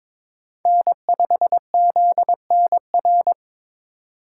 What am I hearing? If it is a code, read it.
Morse code, heard as a keyed tone, about 21 wpm, 700 Hz. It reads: N5ZNR
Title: USA Call Signs - 5 Characters - 4x Character Spacing - 22wpm